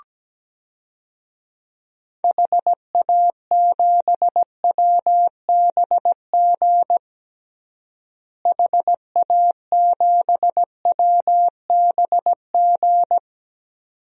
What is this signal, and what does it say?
Morse code, heard as a keyed tone, about 17 wpm, 700 Hz. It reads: HA7WBG HA7WBG
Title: DXCC-100 Call Signs - 6 Characters - Repeated 2x - 17wpm